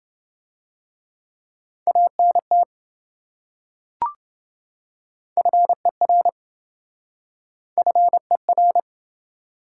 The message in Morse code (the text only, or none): ANT FER FER